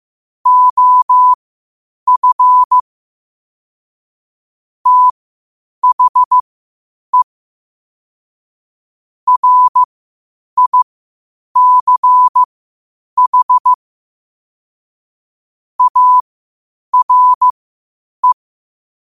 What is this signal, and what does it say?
Morse code, heard as a keyed tone, about 15 wpm, spaced as for 8 wpm, 1000 Hz. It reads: OF THE RICH ARE
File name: qotdfiles / 8wpm